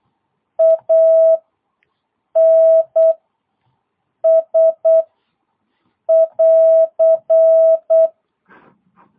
Morse code, heard as a keyed tone, about 8 wpm, 660 Hz. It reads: A N S <AR>